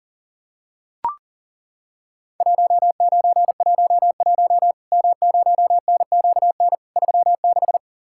Morse code, heard as a keyed tone, about 42 wpm, 700 Hz. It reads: 1911 M0NQN 36
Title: Worked All Europe DX Contest - QTC Exchange - 40wpm